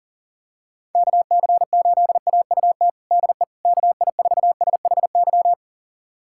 Morse code, heard as a keyed tone, about 40 wpm, 700 Hz. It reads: KC8AUT DE KI4SHY
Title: USA Callsign DE Callsign - 6 Characters - 40wpm